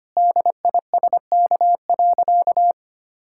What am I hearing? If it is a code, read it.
Morse code, heard as a keyed tone, about 25 wpm, 700 Hz. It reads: DISK.
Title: Sentences from Top 1000 Words - Review Part 6 - 25wpm